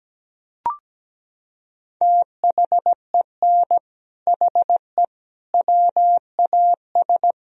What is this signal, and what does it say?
Morse code, heard as a keyed tone, about 17 wpm, 700 Hz. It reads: THEN HE WAS
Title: Sentences from Top 1000 Words - Review Part 2 - 17wpm